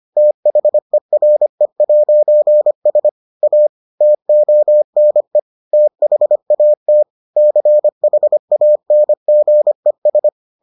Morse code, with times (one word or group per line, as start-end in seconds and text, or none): 0.2-3.1 THERE'S
3.4-3.7 A
4.0-5.4 TONE
5.7-7.0 THAT
7.4-10.3 CHANGES